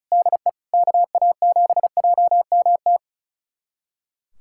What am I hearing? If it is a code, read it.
Morse code, heard as a keyed tone, about 36 wpm, 700 Hz. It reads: DE KA7JMT
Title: USA Callsign DE Callsign 35wpm